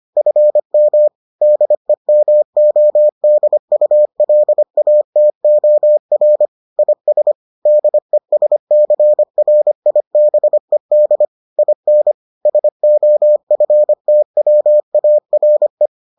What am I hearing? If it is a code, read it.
Morse code, heard as a keyed tone, about 25 wpm, 600 Hz. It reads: FM DEMODULATOR IS DESCRIBED IN SOFTWARE